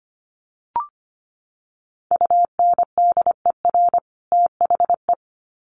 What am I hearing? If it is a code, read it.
Morse code, heard as a keyed tone, about 25 wpm, 700 Hz. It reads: UNDER THE
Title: Sets of 2 Words 25wpm